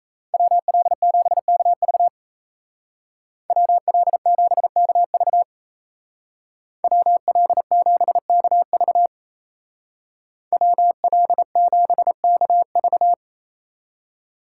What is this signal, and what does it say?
Morse code, heard as a keyed tone, about 34 wpm, 700 Hz. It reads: WL7KV WL7KV WL7KV WL7KV
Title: Speed Racing - USA 2x2 Call Signs 28wpm